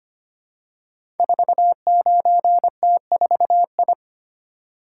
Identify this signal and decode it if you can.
Morse code, heard as a keyed tone, about 25 wpm, 700 Hz. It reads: 49T4I
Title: Code Groups - 5 Letter:Numbers - 25wpm